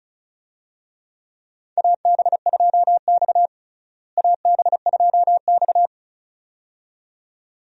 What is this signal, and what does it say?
Morse code, heard as a keyed tone, about 36 wpm, 700 Hz. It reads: AB2X AB2X
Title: USA Call Signs - 4 Characters - Repeated 2x - 35wpm